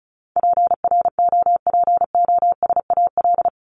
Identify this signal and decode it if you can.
Morse code, heard as a keyed tone, about 34 wpm, 700 Hz. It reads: PROPOSAL
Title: Top 1751-2000 Words 35wpm